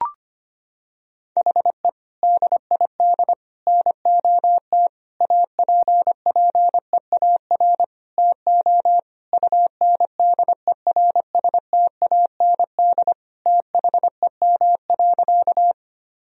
Morse code, heard as a keyed tone, about 25 wpm, 700 Hz. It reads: HE DID NOT APPEAR TO UNDERSTAND THEM.
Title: Sentences from Top 1000 Words - Review Part 2 - 25wpm